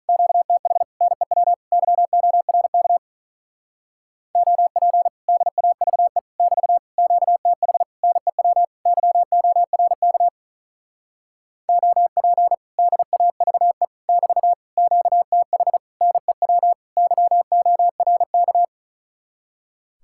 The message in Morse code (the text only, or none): QTH NEW YORK OP DAVE <BT> QTH NEW YORK OP DAVE <BT> QTH NEW YORK